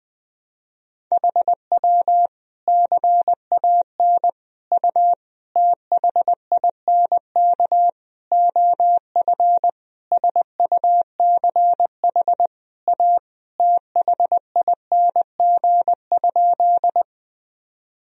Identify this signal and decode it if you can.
Morse code, heard as a keyed tone, about 20 wpm, 700 Hz. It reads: HW CAN U THINK OF SUCH A THING?
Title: Sentences from Top 200 Words and Abbreviations 20wpm